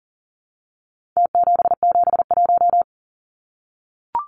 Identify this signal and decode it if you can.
Morse code, heard as a keyed tone, about 40 wpm, 700 Hz. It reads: T771
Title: Numbers - Contest Style - 4-Digit 40wpm